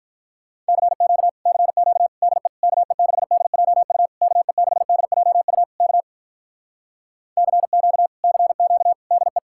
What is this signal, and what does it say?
Morse code, heard as a keyed tone, about 46 wpm, 700 Hz. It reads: CQ CQ DE KE6DJU KE6DJU K CQ CQ DE